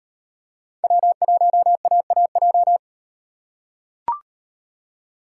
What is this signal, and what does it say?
Morse code, heard as a keyed tone, about 38 wpm, 700 Hz. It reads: W1AAJ